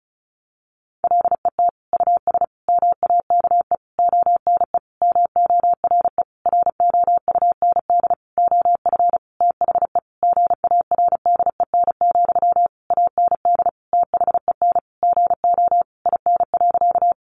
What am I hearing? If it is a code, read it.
Morse code, heard as a keyed tone, about 34 wpm, 700 Hz. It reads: LET US MAKE ONE MORE ROUND OF THE GARDEN, AND THEN GO IN.
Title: Sentences from Top 700 Words 35wpm